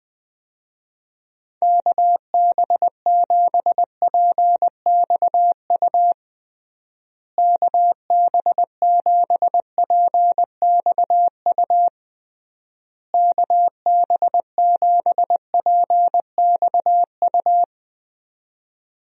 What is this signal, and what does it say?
Morse code, heard as a keyed tone, about 21 wpm, 700 Hz. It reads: KB7PXU KB7PXU KB7PXU
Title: USA Call Signs - 6 Characters - Repeated 3x - 20wpm